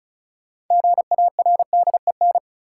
Morse code, heard as a keyed tone, about 36 wpm, 700 Hz. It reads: GARDEN
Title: Sentences from Top 700 Words - New Word Drill - 35wpm